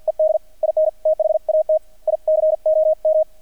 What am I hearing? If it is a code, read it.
Morse code, heard as a keyed tone, about 50 wpm, 630 Hz, with radio noise. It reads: ED IN THAT ICOM
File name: archived / CRSnet